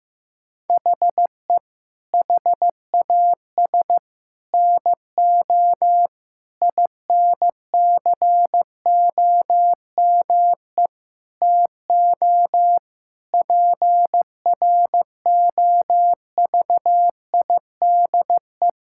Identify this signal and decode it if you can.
Morse code, heard as a keyed tone, about 16 wpm, 700 Hz. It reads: HE HAS NO INCOME TO PROVIDE
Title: Sentences from Top 1000 Words - Review Part 5 - 15wpm